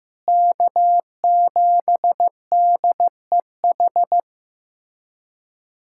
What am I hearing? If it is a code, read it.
Morse code, heard as a keyed tone, about 16 wpm, 700 Hz. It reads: K7DEH
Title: USA Callsign DE Callsign - 5 Characters - 15wpm